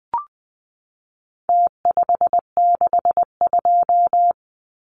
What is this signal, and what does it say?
Morse code, heard as a keyed tone, about 20 wpm, 700 Hz. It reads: T562
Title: Numbers - Contest Style - 4-Digit 20wpm